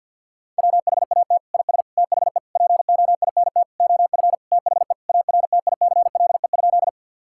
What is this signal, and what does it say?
Morse code, heard as a keyed tone, about 49 wpm, 700 Hz. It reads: WHAT IS THE POINT OF THE ARTICLE?
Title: Sentences from Top 1000 Words - Review Part 5 - 50wpm